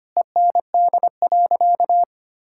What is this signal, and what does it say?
Morse code, heard as a keyed tone, about 25 wpm, 700 Hz. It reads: END.
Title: Sentences from Top 1000 Words - Review Part 6 - 25wpm